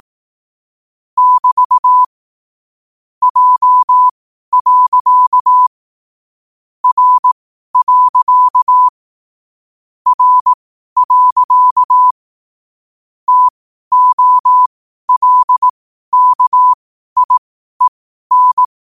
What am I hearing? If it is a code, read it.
Morse code, heard as a keyed tone, about 18 wpm, spaced as for 12 wpm, 1000 Hz. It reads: <BT> J. R. R. TOLKIEN
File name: qotdfiles / 12wpm